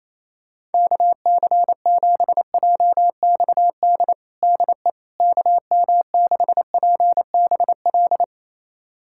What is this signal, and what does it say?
Morse code, heard as a keyed tone, about 28 wpm, 700 Hz. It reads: KC7JXD DE KM6PBL
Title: USA Callsign DE Callsign - 6 Characters - 28wpm